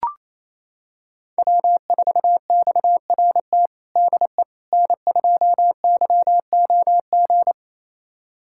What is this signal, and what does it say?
Morse code, heard as a keyed tone, about 28 wpm, 700 Hz. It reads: W4XRT DE N2YOG
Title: USA Callsign DE Callsign 28wpm